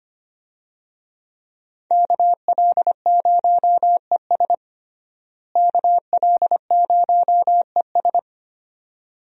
Morse code, heard as a keyed tone, about 25 wpm, 700 Hz. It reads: KL0ES KL0ES
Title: USA Call Signs - 5 Characters - Repeated 2x - 25wpm